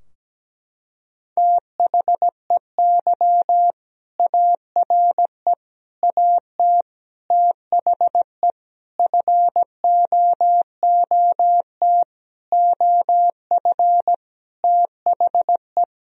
Morse code, heard as a keyed tone, about 17 wpm, 700 Hz. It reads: THEY ARE AT THE FOOT OF THE
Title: Sentences from Top 1000 Words - Review Part 4 - 17wpm